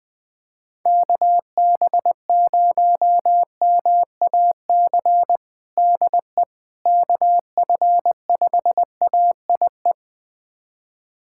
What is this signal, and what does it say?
Morse code, heard as a keyed tone, about 21 wpm, 700 Hz. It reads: KB0MAC DE KF5AIE
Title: USA Callsign DE Callsign - 6 Characters - 20wpm